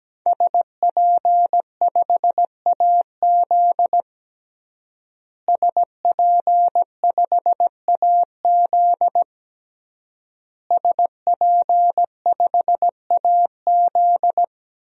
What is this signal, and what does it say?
Morse code, heard as a keyed tone, about 17 wpm, 700 Hz. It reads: SP5AZ SP5AZ SP5AZ
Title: DXCC-100 Call Signs - 5 Characters - Repeated 3x - 17wpm